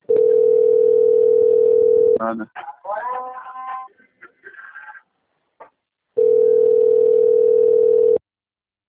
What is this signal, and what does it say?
Ringback tone (North America), 440 + 480 Hz, with speech.
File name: out- - - - - .2013